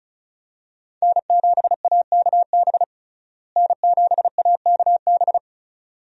Morse code, heard as a keyed tone, about 34 wpm, 700 Hz. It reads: N7AKB N7AKB
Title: USA Call Signs - 5 Characters - Repeated 2x - 35wpm